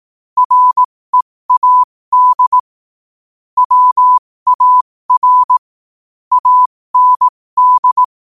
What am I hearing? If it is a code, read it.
Morse code, heard as a keyed tone, about 18 wpm, spaced as for 15 wpm, 1000 Hz. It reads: READ WAR AND
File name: qotdfiles / 15wpm